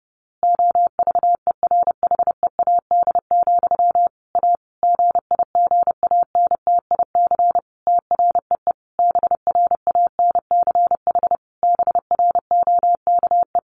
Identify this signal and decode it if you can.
Morse code, heard as a keyed tone, about 30 wpm, 700 Hz. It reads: OVERHEAD, A GIGANTIC TREE BRANCH BROKE
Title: A Sound Of Thunder 30wpm